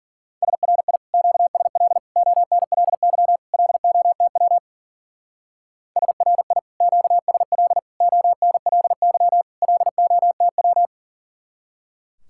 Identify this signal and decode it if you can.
Morse code, heard as a keyed tone, about 48 wpm, 700 Hz. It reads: SRI QSL ONLY LOTW SRI QSL ONLY LOTW
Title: Speed-Racing Common QSO Phrases - 40wpm